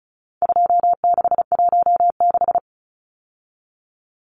Morse code, heard as a keyed tone, about 34 wpm, 700 Hz. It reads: 2616
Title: Numbers - 4-Digit 35wpm